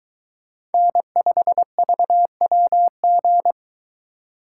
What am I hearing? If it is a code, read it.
Morse code, heard as a keyed tone, about 22 wpm, 700 Hz. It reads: N5VWG